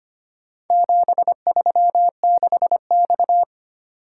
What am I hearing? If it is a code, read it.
Morse code, heard as a keyed tone, about 25 wpm, 700 Hz. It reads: 736X
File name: practice / 25wpm / Four Letter-Number Combinations - 25wpm